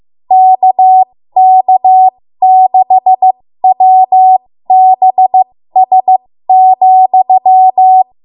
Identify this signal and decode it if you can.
Morse code, heard as a keyed tone, about 15 wpm, spaced as for 13 wpm, 750 Hz. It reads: KK6WBS,
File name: Morse / 13WPM